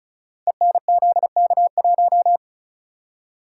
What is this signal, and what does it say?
Morse code, heard as a keyed tone, about 36 wpm, 700 Hz. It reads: ENZK1
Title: Code Groups - 5 Letter:Numbers - 35wpm